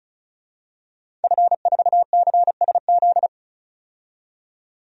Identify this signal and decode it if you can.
Morse code, heard as a keyed tone, about 36 wpm, 700 Hz. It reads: F4CSZ
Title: DXCC-100 Call Signs - 5 Characters - 35wpm